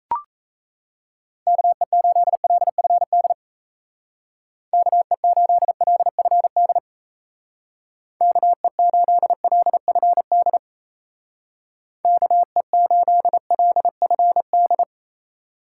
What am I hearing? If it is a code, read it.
Morse code, heard as a keyed tone, about 35 wpm, 700 Hz. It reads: KE8LFD KE8LFD KE8LFD KE8LFD